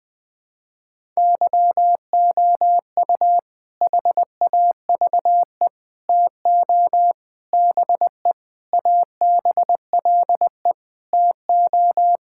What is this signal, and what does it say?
Morse code, heard as a keyed tone, about 20 wpm, 700 Hz. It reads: YOU HAVE TO BE ABLE TO
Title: Sentences with 5-Word Phrases - Spoken Beginning - 20wpm